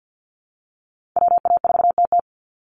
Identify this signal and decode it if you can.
Morse code, heard as a keyed tone, about 50 wpm, 700 Hz. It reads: WA4TT
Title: Call Signs Repeated 3x - 50wpm